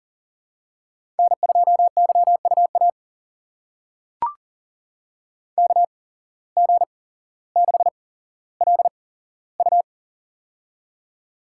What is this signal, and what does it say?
Morse code, heard as a keyed tone, about 41 wpm, spaced as for 17 wpm, 700 Hz. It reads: N2YUA KG6LU
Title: USA Call Signs - 8x Character Spacing - 40wpm